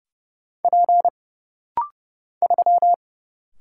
Morse code, heard as a keyed tone, about 30 wpm, 700 Hz. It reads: P 3
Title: Single Letter-Number - Rapid-Fire 30wpm